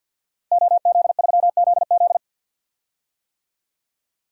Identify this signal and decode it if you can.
Morse code, heard as a keyed tone, about 46 wpm, 700 Hz. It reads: OZ3ZZ
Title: DXCC-100 Call Signs - 5 Characters - Repeated 3x - 50wpm